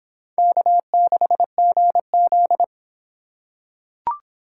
Morse code, heard as a keyed tone, about 26 wpm, 700 Hz. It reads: K6GZ